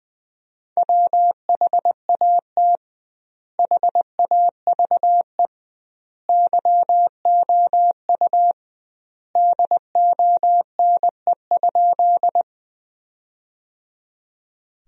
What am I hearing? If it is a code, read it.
Morse code, heard as a keyed tone, about 20 wpm, 700 Hz. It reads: WHAT HAVE YOU DONE?
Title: 2x Word-Spacing - Sentences from Top 500 Words - 20wpm